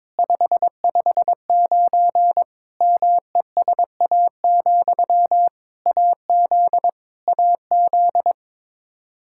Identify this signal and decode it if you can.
Morse code, heard as a keyed tone, about 21 wpm, 700 Hz. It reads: 559 MESA, AZ AZ
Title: Practice QSO - Exchange 1 - Beginning - 22wpm